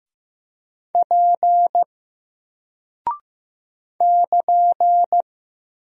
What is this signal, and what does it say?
Morse code, heard as a keyed tone, about 15 wpm, 700 Hz. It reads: P <KN>
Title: Single Character (Letter:Number:Punctuation:Prosign) 15wpm